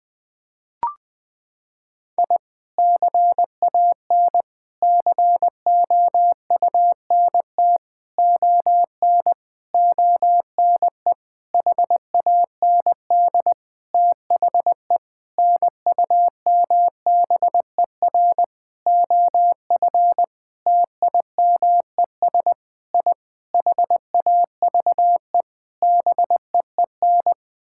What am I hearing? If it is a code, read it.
Morse code, heard as a keyed tone, about 21 wpm, 700 Hz. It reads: I CAN COUNT ON ONE HAND THE NUMBER OF TIMES I HAVE BEEN
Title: Sentences with 9-Word Phrases - 20wpm